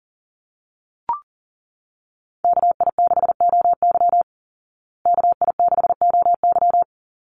Morse code, heard as a keyed tone, about 37 wpm, 700 Hz. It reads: KI6OY KI6OY
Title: Call Signs Repeated 2x - 40wpm